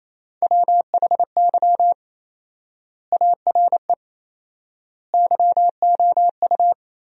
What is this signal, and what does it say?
Morse code, heard as a keyed tone, about 28 wpm, 700 Hz. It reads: WHY ARE YOU
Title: 4x Word-Spacing - Sentences from Top 200 Words - 28wpm